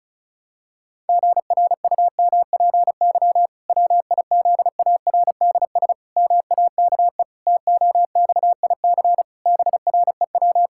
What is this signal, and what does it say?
Morse code, heard as a keyed tone, about 36 wpm, 700 Hz. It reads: GRUMPY WIZARDS MAKE TOXIC BREW
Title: Pangram Challenge v2 - 35wpm